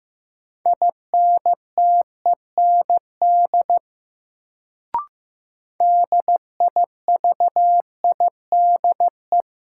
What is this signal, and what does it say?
Morse code, heard as a keyed tone, about 15 wpm, 700 Hz. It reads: INTEND DIVIDE
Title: Sentences from Top 1600 Words - New Word Drill - 15wpm